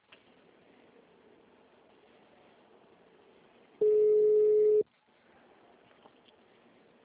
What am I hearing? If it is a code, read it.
Ringback tone (Europe, 425 Hz).